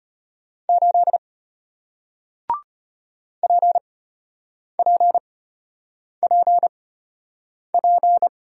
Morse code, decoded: 8 PPPP

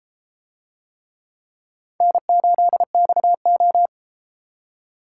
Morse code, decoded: N8XO

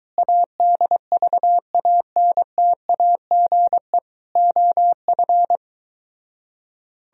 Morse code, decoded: ADVANTAGE OF

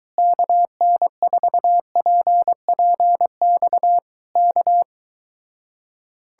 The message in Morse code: KN4PPX K

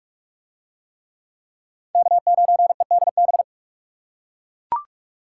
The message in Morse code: K9EDB

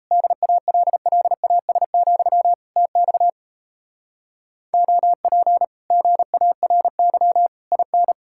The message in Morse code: DALLAS, TX OP GARY IN

